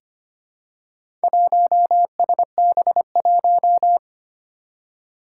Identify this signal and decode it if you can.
Morse code, heard as a keyed tone, about 25 wpm, 700 Hz. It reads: 1SB1
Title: Four Challenging Letter-Number Combinations 25wpm